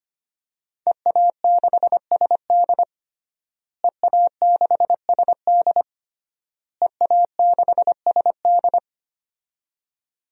Morse code, heard as a keyed tone, about 25 wpm, 700 Hz. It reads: EA6SD EA6SD EA6SD
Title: DXCC-100 Call Signs - 5 Characters - Repeated 3x - 25wpm